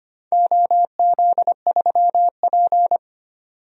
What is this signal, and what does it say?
Morse code, heard as a keyed tone, about 25 wpm, 700 Hz. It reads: OZ3P
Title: DXCC-100 Call Signs - 4 Characters - 25wpm